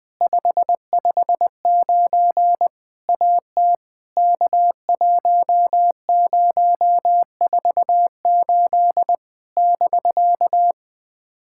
Morse code, heard as a keyed tone, about 20 wpm, 700 Hz. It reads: 559 AT K1048 <BK>
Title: POTA QSO Practice - 20wpm